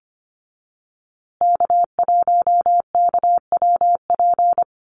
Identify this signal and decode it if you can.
Morse code, heard as a keyed tone, about 25 wpm, 700 Hz. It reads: K1KWP
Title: USA Call Signs Repeated 3x 25wpm